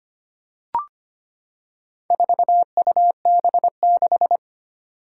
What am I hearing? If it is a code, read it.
Morse code, heard as a keyed tone, about 25 wpm, 700 Hz. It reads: 4UB6
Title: Four Challenging Letter-Number Combinations 25wpm